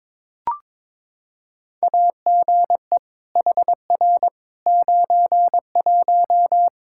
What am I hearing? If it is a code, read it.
Morse code, heard as a keyed tone, about 22 wpm, 700 Hz. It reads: AGE HR 91
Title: Practice QSO - Exchange 3 - More Info - 22wpm